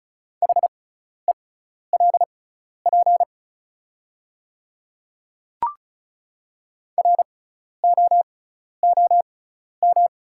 Morse code, heard as a keyed tone, about 34 wpm, spaced as for 12 wpm, 700 Hz. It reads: HELP ROOM
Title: 6x Character Spacing - Top 500 Words - 35wpm